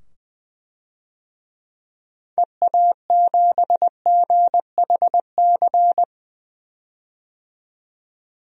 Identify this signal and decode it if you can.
Morse code, heard as a keyed tone, about 20 wpm, 700 Hz. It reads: EA7GHC
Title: DXCC-100 Call Signs - 6 Characters - 20wpm